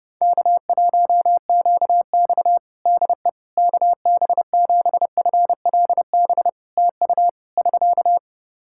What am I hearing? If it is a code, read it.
Morse code, heard as a keyed tone, about 30 wpm, 700 Hz. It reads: K1QX DE KB7FLB TU <SK>
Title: Practice QSO - Exchange 4 - Ending - 30wpm